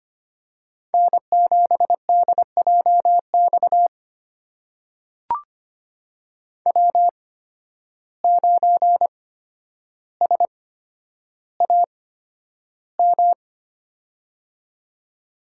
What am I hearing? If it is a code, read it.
Morse code, heard as a keyed tone, about 25 wpm, spaced as for 11 wpm, 700 Hz. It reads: N7DJX W9SAM
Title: USA Call Signs - 5 Characters - 8x Character Spacing - 25wpm